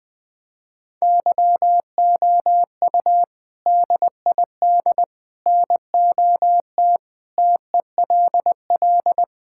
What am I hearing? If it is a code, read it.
Morse code, heard as a keyed tone, about 20 wpm, 700 Hz. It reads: YOU DID NOT TELL
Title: Sentences with 3-Word Phrases - 20wpm